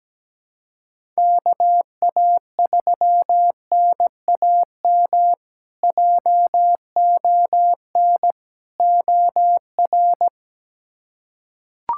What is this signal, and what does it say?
Morse code, heard as a keyed tone, about 17 wpm, 700 Hz. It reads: KA3NAM JON OR
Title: K1USN SST QSO Practice - 17wpm